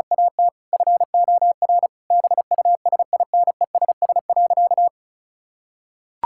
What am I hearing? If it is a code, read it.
Morse code, heard as a keyed tone, about 36 wpm, 700 Hz. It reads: AT FOR BUSINESS.
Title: Sentences from Top 1000 Words - Review Part 5 - 35wpm